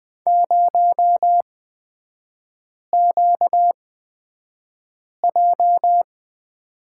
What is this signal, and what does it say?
Morse code, heard as a keyed tone, about 20 wpm, spaced as for 5 wpm, 700 Hz. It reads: 0QJ
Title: 236 Lesson 38e - Letter q - Three Characters 20wpm